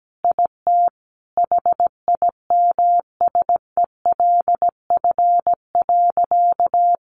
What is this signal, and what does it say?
Morse code, heard as a keyed tone, about 17 wpm, 700 Hz. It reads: IT HIMSELF.
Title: Additional Easy Phrases 17wpm